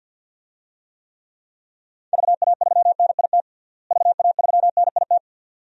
Morse code, heard as a keyed tone, about 49 wpm, 700 Hz. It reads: VA3NIT VA3NIT
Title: DXCC-100 Call Signs - 6 Characters - Repeated 2x - 50wpm